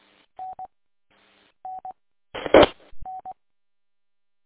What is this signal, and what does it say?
There is no code.